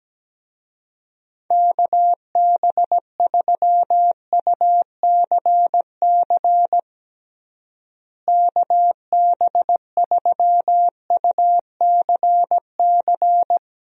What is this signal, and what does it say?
Morse code, heard as a keyed tone, about 17 wpm, 700 Hz. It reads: KB3UCC KB3UCC
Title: USA Call Signs - 6 Characters - Repeated 3x - 17wpm